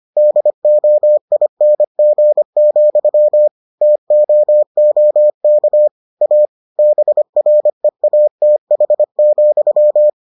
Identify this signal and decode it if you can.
Morse code, heard as a keyed tone, about 25 wpm, 600 Hz. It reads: DOING, TOOK A BREATH,